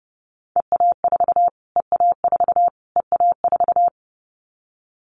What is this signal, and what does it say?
Morse code, heard as a keyed tone, about 30 wpm, 700 Hz. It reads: EA4 EA4 EA4
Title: Callsign Prefixes for Top 25 DXCC - Repeated 3x - 30wpm